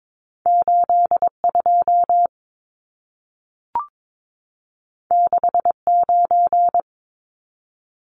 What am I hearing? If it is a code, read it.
Morse code, heard as a keyed tone, about 22 wpm, 700 Hz. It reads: 82 69
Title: Numbers - 2-Digit 22wpm